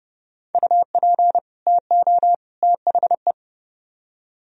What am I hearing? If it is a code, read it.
Morse code, heard as a keyed tone, about 29 wpm, 700 Hz. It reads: UP TO THE